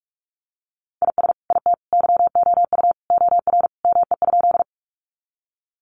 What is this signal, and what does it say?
Morse code, heard as a keyed tone, about 44 wpm, 700 Hz. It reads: IS IT YOU OR ME?
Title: Sentences from Top 200 Words 45wpm